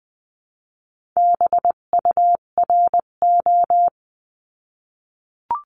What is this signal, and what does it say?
Morse code, heard as a keyed tone, about 20 wpm, 700 Hz. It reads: BURO
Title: 258 Lesson 42c - Words 16-20wpm